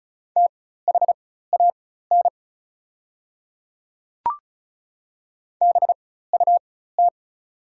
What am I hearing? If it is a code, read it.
Morse code, heard as a keyed tone, about 36 wpm, spaced as for 16 wpm, 690 Hz. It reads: THAN BUT
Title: 4x Character Spacing - Top 100 Words - 35wpm